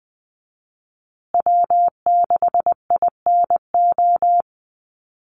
Morse code, heard as a keyed tone, about 20 wpm, 700 Hz. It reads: W6INO
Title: USA Call Signs Repeated 2x 20wpm